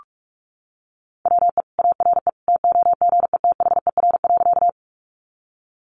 Morse code, heard as a keyed tone, about 44 wpm, 700 Hz. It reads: WE ARE TOGETHER.
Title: Sentences from Top 300 Words 45wpm